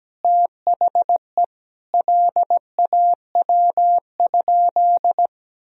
Morse code, heard as a keyed tone, about 17 wpm, 700 Hz. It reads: THE LAW?
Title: Sentences from Top 1000 Words - Review Part 4 - 17wpm